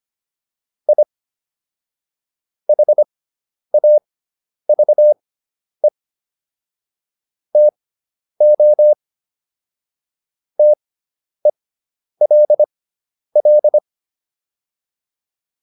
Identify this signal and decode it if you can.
Morse code, heard as a keyed tone, about 25 wpm, spaced as for 10 wpm, 600 Hz. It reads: I HAVE TO TELL
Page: intermediate-practice-files